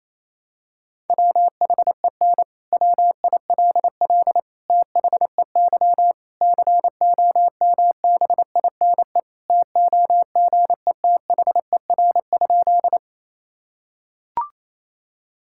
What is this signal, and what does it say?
Morse code, heard as a keyed tone, about 28 wpm, 700 Hz. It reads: WHEN WILL THEY COMBINE TOGETHER?
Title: Sentences from Top 1600 Words 28wpm